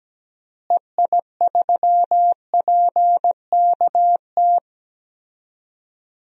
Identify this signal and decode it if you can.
Morse code, heard as a keyed tone, about 17 wpm, 700 Hz. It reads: EI3PKT
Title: DXCC-100 Call Signs - 6 Characters - Repeated 2x - 17wpm